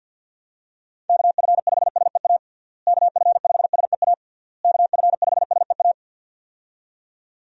Morse code, heard as a keyed tone, about 47 wpm, 700 Hz. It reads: KF5SEA KF5SEA KF5SEA
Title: USA Call Signs - 6 Characters - Repeated 3x - 50wpm